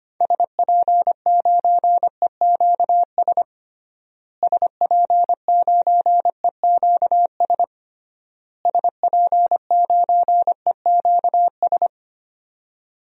Morse code, heard as a keyed tone, about 25 wpm, 700 Hz. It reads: SP9EQS SP9EQS SP9EQS
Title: DXCC-100 Call Signs - 6 Characters - Repeated 3x - 25wpm